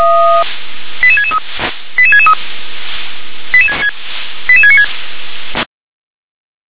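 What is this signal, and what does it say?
Five-tone selective call (ZVEI), heard as quick runs of five tones, tones of about 70 ms. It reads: 88663 88663 88696
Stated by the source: ZVEI five-tone call 88718 received by SDR.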